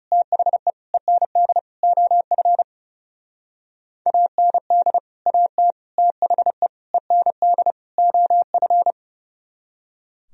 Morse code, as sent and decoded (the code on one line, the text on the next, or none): - .... / . -. -.. / --- ..-. / .- -. -.. / .- - / - .... / . -. -.. / --- ..-.
THE END OF AND AT THE END OF